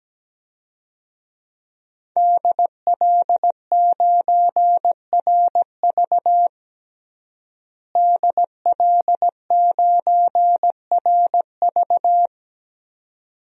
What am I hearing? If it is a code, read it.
Morse code, heard as a keyed tone, about 17 wpm, 700 Hz. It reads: DL9RV DL9RV
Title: DXCC-100 Call Signs - 5 Characters - Repeated 2x - 17wpm